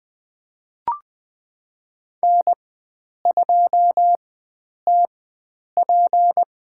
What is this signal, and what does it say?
Morse code, heard as a keyed tone, about 20 wpm, spaced as for 9 wpm, 700 Hz. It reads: N2TP
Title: USA Call Signs - 4x Character Spacing - 20wpm